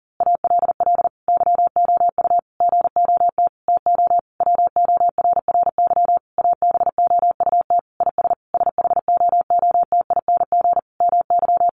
Morse code, heard as a keyed tone, about 40 wpm, 700 Hz. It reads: ALL YOU GOT TO WORRY ABOUT IS SHOOTING MY
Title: A Sound Of Thunder 40wpm